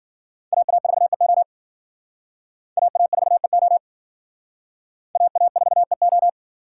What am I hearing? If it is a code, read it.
Morse code, heard as a keyed tone, about 50 wpm, 700 Hz. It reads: AA4EO AA4EO AA4EO
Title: Speed Racing - USA 2x2 Call Signs 40wpm